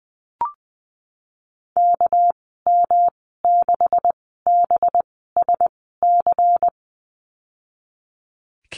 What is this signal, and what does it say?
Morse code, heard as a keyed tone, about 20 wpm, spaced as for 14 wpm, 700 Hz. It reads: KM6BSC
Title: 219 Lesson 35g - Number 0 - Call Signs 20wpm